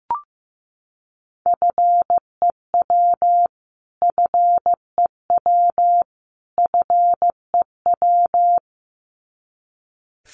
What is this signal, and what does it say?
Morse code, heard as a keyed tone, about 14 wpm, 700 Hz. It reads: FEW FEW FEW
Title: Top 300 Words Repeated 15wpm